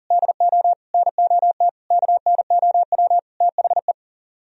Morse code, heard as a keyed tone, about 49 wpm, 700 Hz. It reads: DO NOT KNOW THE
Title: Speed-Racing - Sets of 5 Words 40wpm